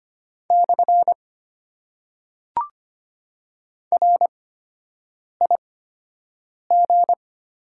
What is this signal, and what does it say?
Morse code, heard as a keyed tone, about 25 wpm, spaced as for 7 wpm, 690 Hz. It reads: / RIG
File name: Top 90 QSO Elements - 8x Character Spacing - 25wpm